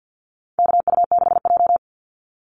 Morse code, heard as a keyed tone, about 49 wpm, 700 Hz. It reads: KU6J
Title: NA SOTA Chaser Callsigns 50wpm